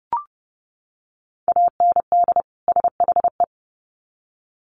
Morse code, heard as a keyed tone, about 30 wpm, 700 Hz. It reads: AND SHE
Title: Sets of 2 Words 30wpm